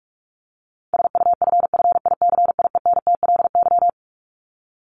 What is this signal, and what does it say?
Morse code, heard as a keyed tone, about 44 wpm, 700 Hz. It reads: SUFFICIENTLY
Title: LY Suffix - Top 100 Words with Base Words 45wpm